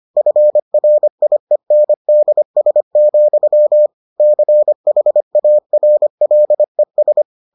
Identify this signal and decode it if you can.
Morse code, heard as a keyed tone, about 25 wpm, 600 Hz. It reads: FRIENDS, CHARLES